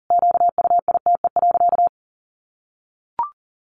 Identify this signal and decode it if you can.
Morse code, heard as a keyed tone, about 40 wpm, 700 Hz. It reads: QUITE.
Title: Sentences from Top 1300 Words 40wpm